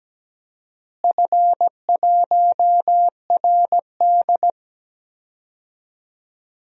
Morse code, heard as a keyed tone, about 17 wpm, 700 Hz. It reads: F1RD
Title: DXCC-100 Call Signs - 4 Characters - Repeated 2x - 17wpm